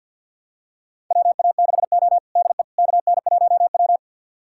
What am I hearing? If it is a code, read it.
Morse code, heard as a keyed tone, about 52 wpm, 700 Hz. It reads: WA6O DE KN1W
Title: USA Callsign DE Callsign - 4 Characters - 50wpm